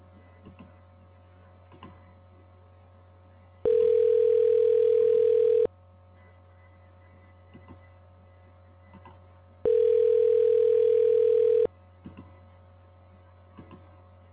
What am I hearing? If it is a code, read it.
Ringback tone (North America), 440 + 480 Hz.
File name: recordings / RE43605ac886f a29b8e9b51f2